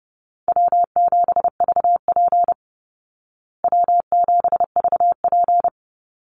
Morse code, heard as a keyed tone, about 29 wpm, 700 Hz. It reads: W7VP W7VP
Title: Easy Call Signs Repeated 3x - 30wpm